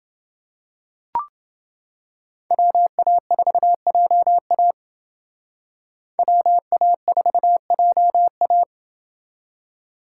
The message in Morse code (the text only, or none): WA4JA WA4JA